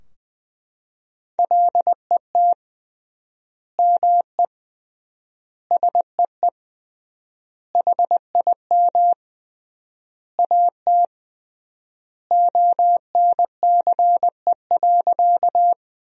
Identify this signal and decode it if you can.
Morse code, heard as a keyed tone, about 20 wpm, 700 Hz. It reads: LET ME SEE HIM AT ONCE.
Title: 3x Word-Spacing - Sentences from Top 300 Words - 20wpm